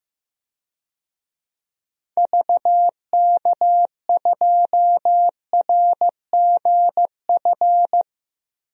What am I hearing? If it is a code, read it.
Morse code, heard as a keyed tone, about 15 wpm, 700 Hz. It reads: VK2RGF
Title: DXCC-100 Call Signs - 6 Characters - 15wpm